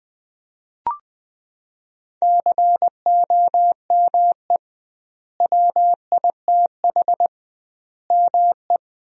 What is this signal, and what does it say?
Morse code, heard as a keyed tone, about 20 wpm, 700 Hz. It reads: COME WITH ME
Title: 2x Word-Spacing - Sentences from Top 200 Words - 20wpm